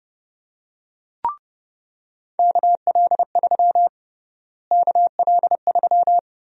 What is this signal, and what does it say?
Morse code, heard as a keyed tone, about 30 wpm, 700 Hz. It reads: KL3 KL3
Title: USA Call Sign Prefixes - Repeated 2x - 30wpm